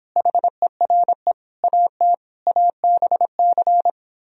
Morse code, heard as a keyed tone, about 29 wpm, 700 Hz. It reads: HERE AT ABC